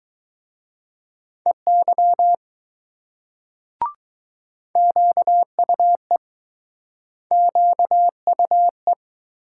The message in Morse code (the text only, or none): EY QUE QUE